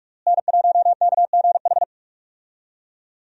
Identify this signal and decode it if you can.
Morse code, heard as a keyed tone, about 46 wpm, 700 Hz. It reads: N1KGH
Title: USA Call Signs - 5 Characters - Repeated 2x - 45wpm